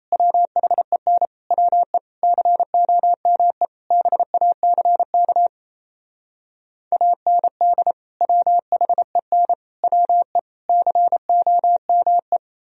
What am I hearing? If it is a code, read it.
Morse code, heard as a keyed tone, about 37 wpm, 700 Hz. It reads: WHEN WE COME BACK AND WHEN WE COME